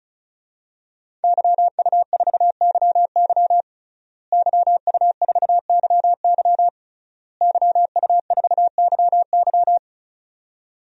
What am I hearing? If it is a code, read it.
Morse code, heard as a keyed tone, about 34 wpm, 700 Hz. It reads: YU4YY YU4YY YU4YY
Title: DXCC-100 Call Signs - 5 Characters - Repeated 3x - 35wpm